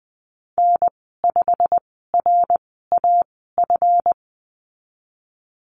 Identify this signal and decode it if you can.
Morse code, heard as a keyed tone, about 20 wpm, spaced as for 15 wpm, 700 Hz. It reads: N5RAF
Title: 161 Lesson 26g - Letter g - Call Signs 20wpm